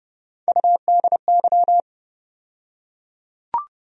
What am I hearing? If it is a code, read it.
Morse code, heard as a keyed tone, about 30 wpm, 700 Hz. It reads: UDY
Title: USA Call Sign Postfixes - 3 Letters 30wpm